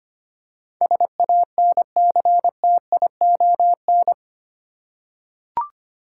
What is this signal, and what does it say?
Morse code, heard as a keyed tone, about 25 wpm, 690 Hz. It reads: SANCTION
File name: 8 Letter Words - First 3 Letters Given - 25wpm